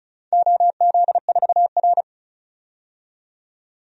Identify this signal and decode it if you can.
Morse code, heard as a keyed tone, about 36 wpm, 700 Hz. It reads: OZ4R
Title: DXCC-100 Call Signs - 4 Characters - 35wpm